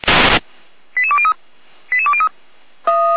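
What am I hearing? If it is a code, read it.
Five-tone selective call (ZVEI), heard as quick runs of five tones, tones of about 70 ms. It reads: 88203 88203
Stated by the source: ZVEI five-tone selective call received on SDR, recorded by Feuerwehr Orlamünde.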